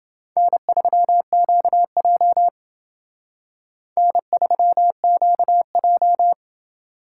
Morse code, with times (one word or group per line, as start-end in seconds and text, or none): 0.4-2.5 N3QJ
4.0-6.3 N3QJ